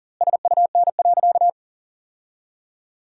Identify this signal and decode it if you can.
Morse code, heard as a keyed tone, about 40 wpm, 700 Hz. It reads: SUN.
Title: 2x Word-Spacing - Sentences from Top 500 Words - 40wpm